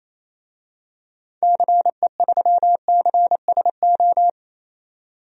Morse code, heard as a keyed tone, about 27 wpm, 700 Hz. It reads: CE3CSO
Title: DXCC-100 Call Signs - 6 Characters - 28wpm